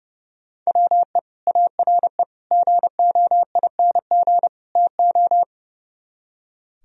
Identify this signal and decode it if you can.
Morse code, heard as a keyed tone, about 35 wpm, 700 Hz. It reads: WE ARE GOING TO
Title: Speed-Racing - Sets of 4 Words 30wpm